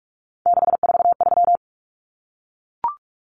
Morse code, heard as a keyed tone, about 44 wpm, 700 Hz. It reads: 643
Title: Common Cut Numbers - 3-Digit 45wpm